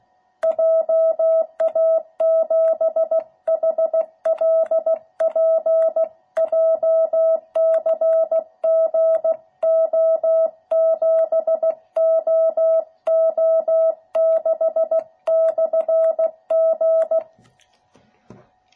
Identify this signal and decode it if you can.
Morse code, heard as a keyed tone, about 16 wpm, 660 Hz. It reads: JA7HLP JCGO7OO6/G